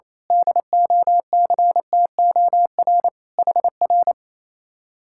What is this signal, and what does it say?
Morse code, heard as a keyed tone, about 26 wpm, 700 Hz. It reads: DOCTOR HR
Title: Speed-Racing Common QSO Phrases - 28wpm